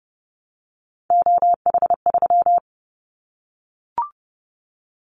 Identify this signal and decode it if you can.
Morse code, heard as a keyed tone, about 30 wpm, 700 Hz. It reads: OH3
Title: Callsign Prefixes for Top 25 DXCC - Repeated 3x - 30wpm